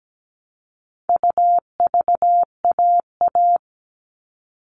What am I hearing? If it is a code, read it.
Morse code, heard as a keyed tone, about 17 wpm, 700 Hz. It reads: UVAA
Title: All Cut Numbers - 4-Digit 17wpm